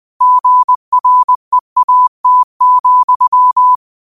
Morse code, heard as a keyed tone, about 20 wpm, 1000 Hz. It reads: GREAT,
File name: qotdfiles / 20wpm